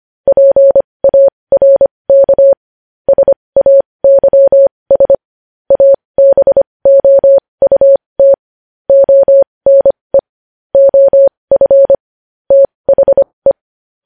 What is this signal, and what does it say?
Morse code, heard as a keyed tone, about 25 wpm, spaced as for 20 wpm, 550 Hz. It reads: PARK SAYS ABOUT ONE OF THE
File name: podcast / News headlines-1x-25x20